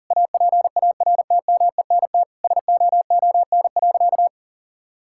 Morse code, heard as a keyed tone, about 40 wpm, 700 Hz. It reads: APARTMENT SOON.
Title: Sentences from Top 1000 Words - Review Part 6 - 40wpm